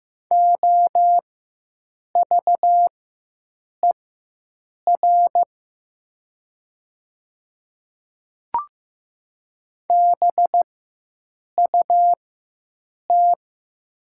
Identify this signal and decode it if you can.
Morse code, heard as a keyed tone, about 15 wpm, spaced as for 7 wpm, 700 Hz. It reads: OVER BUT
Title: 4x Character Spacing - Top 100 Words - 15wpm